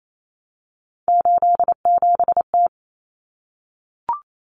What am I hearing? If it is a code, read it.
Morse code, heard as a keyed tone, about 28 wpm, 700 Hz. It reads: 87T
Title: Common Cut Numbers - 3-Digit 28wpm